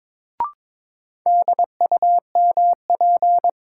Morse code, heard as a keyed tone, about 22 wpm, 700 Hz. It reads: DUMP
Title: Sentences from Top 1300 Words - New Word Drill - 22wpm